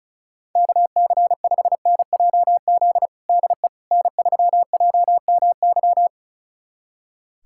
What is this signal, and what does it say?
Morse code, heard as a keyed tone, about 34 wpm, 700 Hz. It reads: KC5NJZ DE N3JMY
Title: USA Callsign DE Callsign 35wpm